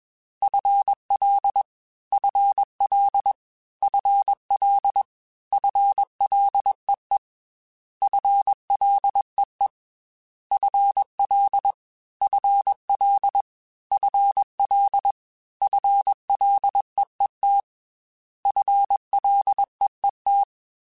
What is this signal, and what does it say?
Morse code, heard as a keyed tone, about 21 wpm, 800 Hz. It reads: FL FL FL FLEE FLEE FL FL FL FLEET FLEET
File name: FL / FL mp3s / FL Prefixes 3x2 @20